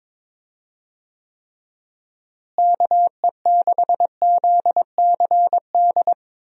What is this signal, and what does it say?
Morse code, heard as a keyed tone, about 22 wpm, 700 Hz. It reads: KE6ZCD
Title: USA Call Signs - 6 Characters - Repeated 2x - 22wpm